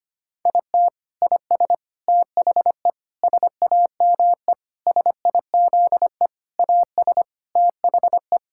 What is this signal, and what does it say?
Morse code, heard as a keyed tone, about 25 wpm, 700 Hz. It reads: IT IS THE SAME SIZE AS THE
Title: Sentences from Top 1000 Words - Review Part 6 - 25wpm